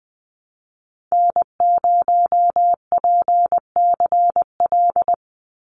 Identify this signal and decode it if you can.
Morse code, heard as a keyed tone, about 20 wpm, 700 Hz. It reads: N0PCL
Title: NA SOTA Activator Callsigns 20wpm